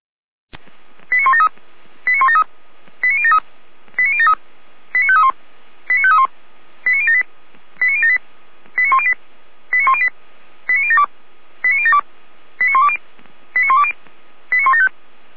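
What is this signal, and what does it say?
Five-tone selective call (ZVEI), heard as quick runs of five tones, tones of about 70 ms. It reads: 89173 89173 89073 89073 89531 89531 89078 89078 89108 89108 89072 89072 89120 89120 89175
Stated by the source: ZVEI five-tone call 89022 received on an SDR, fire brigade Feuerwehr Orlamünde.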